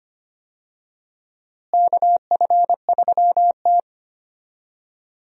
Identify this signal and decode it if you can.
Morse code, heard as a keyed tone, about 25 wpm, 700 Hz. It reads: KF3T
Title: USA Call Signs - 4 Characters - Repeated 3x - 25wpm